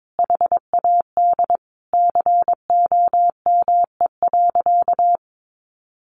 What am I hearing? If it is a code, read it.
Morse code, heard as a keyed tone, about 22 wpm, 700 Hz. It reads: HAD COME.
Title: Sentences from Top 1100 Words 22wpm